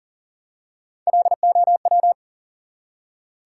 Morse code, heard as a keyed tone, about 40 wpm, 700 Hz. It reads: LOW
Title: Three Letter Words - 40wpm